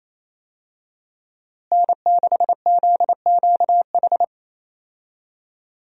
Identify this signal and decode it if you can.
Morse code, heard as a keyed tone, about 28 wpm, 700 Hz. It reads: N6ZQH
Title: USA Call Signs - 5 Characters - Repeated 2x - 28wpm